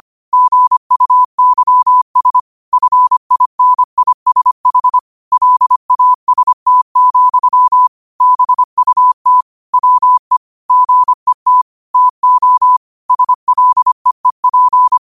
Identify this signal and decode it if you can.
Morse code, heard as a keyed tone, about 25 wpm, 1000 Hz. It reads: GUYS FINISH LAST, BUT WE GET TO SLEEP
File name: qotdfiles / 25wpm